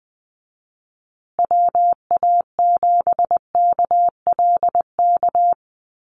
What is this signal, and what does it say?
Morse code, heard as a keyed tone, about 20 wpm, 700 Hz. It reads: WA7KLK
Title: USA Call Signs Repeated 3x 20wpm